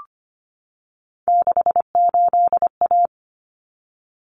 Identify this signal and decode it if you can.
Morse code, heard as a keyed tone, about 25 wpm, 700 Hz. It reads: 68A
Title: Common Cut Numbers - 3-Digit 25wpm